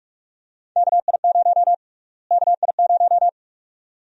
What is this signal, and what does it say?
Morse code, heard as a keyed tone, about 46 wpm, 700 Hz. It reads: KI0 KI0
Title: USA Call Sign Prefixes - Repeated 2x - 45wpm